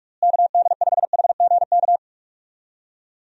Morse code, heard as a keyed tone, about 44 wpm, 700 Hz. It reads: KD5HGK
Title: USA Call Signs - 6 Characters - Repeated 3x - 45wpm